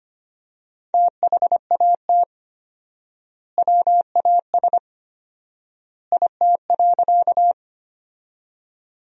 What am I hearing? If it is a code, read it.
Morse code, heard as a keyed tone, about 25 wpm, 700 Hz. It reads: THAT WAS IT.
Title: 4x Word-Spacing - Sentences from Top 100 Words - 25wpm